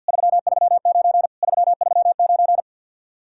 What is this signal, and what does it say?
Morse code, heard as a keyed tone, about 64 wpm, 700 Hz. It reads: 339 339